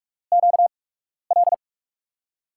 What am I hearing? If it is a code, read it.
Morse code, heard as a keyed tone, about 46 wpm, spaced as for 12 wpm, 700 Hz. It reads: QL
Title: USA Call Signs - 6 Characters - 8x Character Spacing - 45wpm